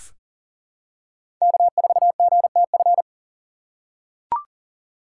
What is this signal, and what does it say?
Morse code, heard as a keyed tone, about 39 wpm, 700 Hz. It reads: K4GTF
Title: USA Call Signs - 5 Characters - 2x Character Spacing - 40wpm